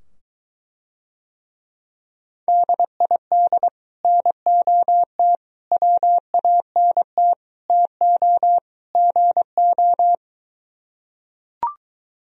Morse code, heard as a keyed tone, about 23 wpm, 700 Hz. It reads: DID NOT WANT TO GO